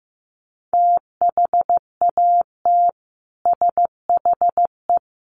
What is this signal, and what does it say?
Morse code, heard as a keyed tone, about 15 wpm, 700 Hz. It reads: THAT SHE
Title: Sets of 3 Words 15wpm